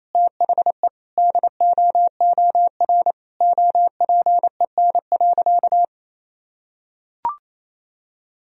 Morse code, heard as a keyed tone, about 28 wpm, 700 Hz. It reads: THE DOOR OPEN.
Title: Sentences from Top 1000 Words - Review Part 1 - 28wpm